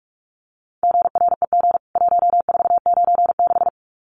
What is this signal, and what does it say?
Morse code, heard as a keyed tone, about 44 wpm, 700 Hz. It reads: GREG 1496
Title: CWT Exchange 45wpm